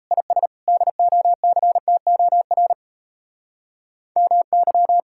Morse code, heard as a keyed tone, about 36 wpm, 700 Hz. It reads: IS DOCTOR MY